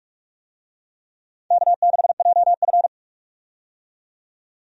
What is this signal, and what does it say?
Morse code, heard as a keyed tone, about 46 wpm, 700 Hz. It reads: K6JF
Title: USA Call Signs - 4 Characters - 45wpm